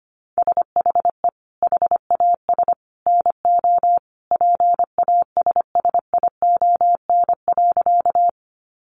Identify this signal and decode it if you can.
Morse code, heard as a keyed tone, about 25 wpm, 700 Hz. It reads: SHE HAS NO PASSION.
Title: Sentences from Top 1300 Words 25wpm